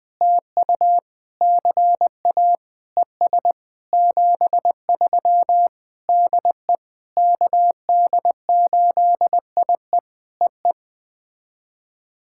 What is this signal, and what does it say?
Morse code, heard as a keyed tone, about 20 wpm, 700 Hz. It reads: TU CA ES 73 DE KD8IE EE
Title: POTA QSO Practice - 20wpm